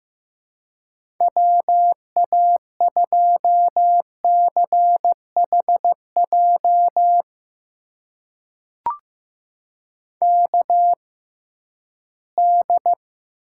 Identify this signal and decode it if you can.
Morse code, heard as a keyed tone, about 15 wpm, spaced as for 11 wpm, 700 Hz. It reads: WA2CHJ KD
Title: USA Call Signs - 6 Characters - 6x Character Spacing - 15wpm